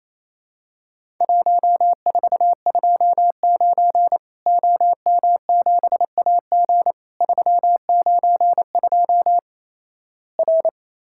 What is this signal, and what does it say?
Morse code, heard as a keyed tone, about 28 wpm, 700 Hz. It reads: 1429 OM7AG 392 R
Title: Worked All Europe DX Contest - QTC Exchange - 28wpm